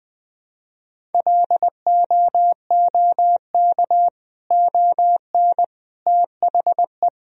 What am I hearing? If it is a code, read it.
Morse code, heard as a keyed tone, about 20 wpm, 700 Hz. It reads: LOOK ON THE
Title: Sentences from Top 1000 Words - Review Part 1 - 20wpm